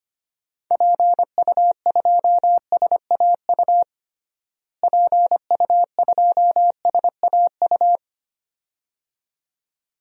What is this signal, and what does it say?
Morse code, heard as a keyed tone, about 25 wpm, 700 Hz. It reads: PU2SAU PU2SAU
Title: DXCC-100 Call Signs - 6 Characters - Repeated 2x - 25wpm